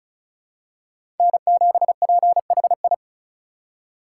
Morse code, heard as a keyed tone, about 34 wpm, 700 Hz. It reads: N7PHI
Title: K1USN SST QSO Practice - 35wpm